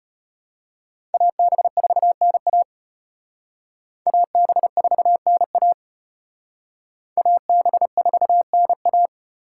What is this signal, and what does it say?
Morse code, heard as a keyed tone, about 32 wpm, 700 Hz. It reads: AB4NA AB4NA AB4NA